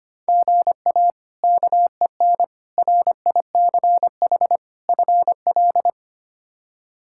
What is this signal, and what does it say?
Morse code, heard as a keyed tone, about 25 wpm, 700 Hz. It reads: GA KEN RICH FL
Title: K1USN SST QSO Practice - 25wpm